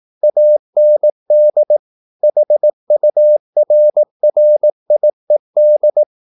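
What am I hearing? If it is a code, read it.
Morse code, heard as a keyed tone, about 18 wpm, 600 Hz. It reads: AND HURRIED
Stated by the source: Morse practice at 18 wpm, English text, from CW med Gnister.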